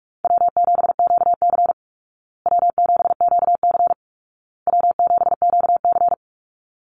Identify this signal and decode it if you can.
Morse code, heard as a keyed tone, about 44 wpm, 700 Hz. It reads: W7QC W7QC W7QC
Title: Easy Call Signs Repeated 3x - 45wpm